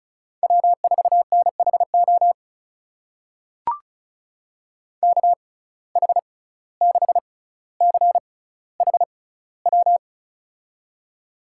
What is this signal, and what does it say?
Morse code, heard as a keyed tone, about 36 wpm, spaced as for 17 wpm, 700 Hz. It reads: W4NHO KH6CHW
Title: USA Call Signs - 6x Character Spacing - 35wpm